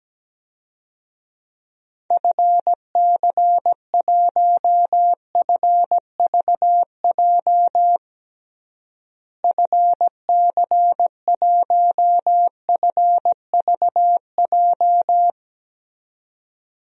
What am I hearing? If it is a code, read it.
Morse code, heard as a keyed tone, about 17 wpm, 700 Hz. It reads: FC1FVJ FC1FVJ
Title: DXCC-100 Call Signs - 6 Characters - Repeated 2x - 17wpm